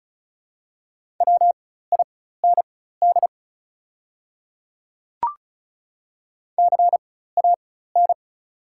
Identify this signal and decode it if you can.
Morse code, heard as a keyed tone, about 36 wpm, spaced as for 16 wpm, 700 Hz. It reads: WIND CAN
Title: 4x Character Spacing - Top 500 Words - 35wpm